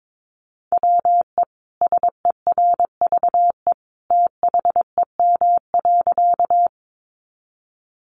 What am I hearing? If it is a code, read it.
Morse code, heard as a keyed tone, about 22 wpm, 700 Hz. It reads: WE SERVE THEM.
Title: Sentences from Top 400 Words 22wpm